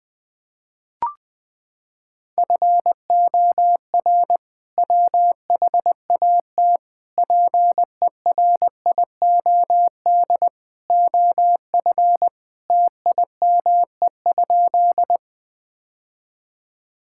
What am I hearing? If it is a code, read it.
Morse code, heard as a keyed tone, about 21 wpm, 700 Hz. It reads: FOR WHAT PERIOD OF TIME?
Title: Sentences from Top 1000 Words - Review Part 5 - 20wpm